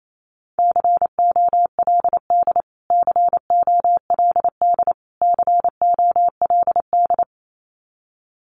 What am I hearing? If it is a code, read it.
Morse code, heard as a keyed tone, about 28 wpm, 700 Hz. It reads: COLD COLD COLD
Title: Top 500 Words Repeated 28wpm